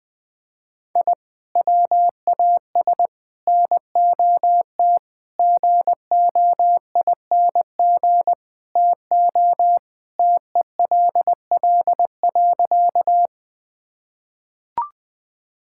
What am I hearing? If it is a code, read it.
Morse code, heard as a keyed tone, about 19 wpm, 700 Hz. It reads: I WAS NOT GOING TO TELL.
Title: Sentences with 5-Word Phrases - 20wpm